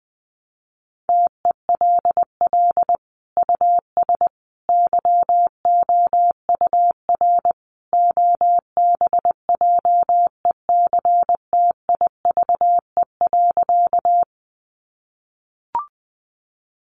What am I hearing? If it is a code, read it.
Morse code, heard as a keyed tone, about 20 wpm, 700 Hz. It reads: TELL US YOUR OBJECTIVE.
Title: Sentences from Top 1200 Words 20wpm